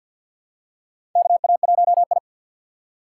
Morse code, heard as a keyed tone, about 52 wpm, 700 Hz. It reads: KA1I
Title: USA Call Signs - 4 Characters - 2x Character Spacing - 50wpm